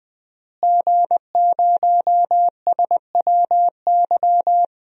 Morse code, heard as a keyed tone, about 20 wpm, 700 Hz. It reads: G0SWY
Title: DXCC-100 Call Signs - 5 Characters - Repeated 2x - 20wpm